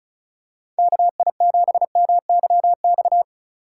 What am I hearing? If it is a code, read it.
Morse code, heard as a keyed tone, about 36 wpm, 700 Hz. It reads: KI7MYX
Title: USA Call Signs - 6 Characters - 4x Character Spacing - 35wpm